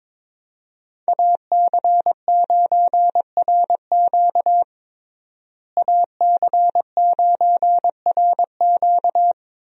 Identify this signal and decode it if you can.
Morse code, heard as a keyed tone, about 21 wpm, 700 Hz. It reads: AC9RQ AC9RQ
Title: USA Call Signs - 5 Characters - Repeated 2x - 22wpm